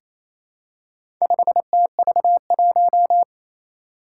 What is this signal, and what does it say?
Morse code, heard as a keyed tone, about 27 wpm, 700 Hz. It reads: 5TV1
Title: Four Challenging Letter-Number Combinations 28wpm